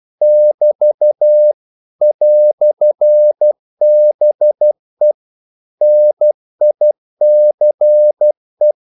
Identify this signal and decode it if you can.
Morse code, heard as a keyed tone, about 12 wpm, 600 Hz. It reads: <BT> "BE NICE
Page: English quotations in CW (telegraphy - morse)